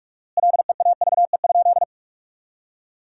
Morse code, heard as a keyed tone, about 46 wpm, 700 Hz. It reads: LEAVE?
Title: 4x Word-Spacing - Sentences from Top 400 Words - 45wpm